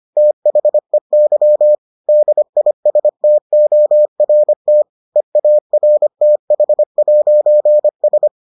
Morse code, heard as a keyed tone, about 25 wpm, 600 Hz. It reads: THEY DISTORT EARTH'S